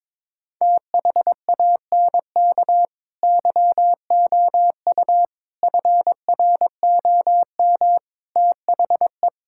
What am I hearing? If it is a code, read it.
Morse code, heard as a keyed tone, about 21 wpm, 700 Hz. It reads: THANK YOU FROM THE
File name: practice / 22wpm / Sets of 8 Words - 22wpm